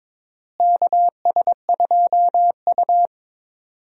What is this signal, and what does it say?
Morse code, heard as a keyed tone, about 22 wpm, 700 Hz. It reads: KS2U
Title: USA Call Signs - 4 Characters - Repeated 2x - 22wpm